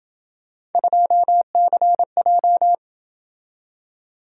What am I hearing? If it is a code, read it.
Morse code, heard as a keyed tone, about 27 wpm, 700 Hz. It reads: 2CJ